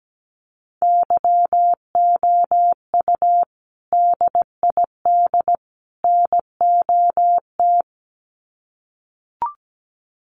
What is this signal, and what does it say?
Morse code, heard as a keyed tone, about 17 wpm, 700 Hz. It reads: YOU DID NOT
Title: Sets of 3 Words 17wpm